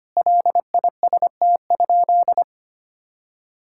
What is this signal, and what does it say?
Morse code, heard as a keyed tone, about 25 wpm, 700 Hz. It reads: LIST?
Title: Sentences from Top 1000 Words - Review Part 6 - 25wpm